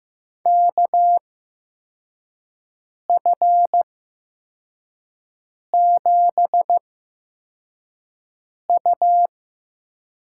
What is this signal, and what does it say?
Morse code, heard as a keyed tone, about 15 wpm, spaced as for 4 wpm, 700 Hz. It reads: KF7U
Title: USA Call Signs - 6 Characters - 8x Character Spacing - 15wpm